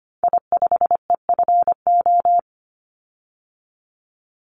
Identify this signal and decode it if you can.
Morse code, heard as a keyed tone, about 25 wpm, 700 Hz. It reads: I5EFO
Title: Call Signs Repeated 3x - 25wpm